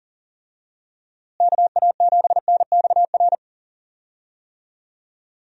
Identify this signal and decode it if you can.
Morse code, heard as a keyed tone, about 39 wpm, 700 Hz. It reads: KA7NXR
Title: USA Call Signs - 6 Characters - Repeated 2x - 40wpm